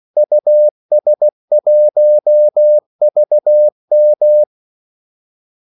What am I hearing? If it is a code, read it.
Morse code, heard as a keyed tone, about 16 wpm, 600 Hz. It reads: US1VM
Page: Kaldesignaler 16 wpm